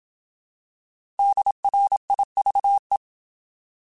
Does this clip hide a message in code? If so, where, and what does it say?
Yes: Morse code from 1.2 to 3.0 s, heard as a keyed tone, about 27 wpm, 800 Hz. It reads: DRIVE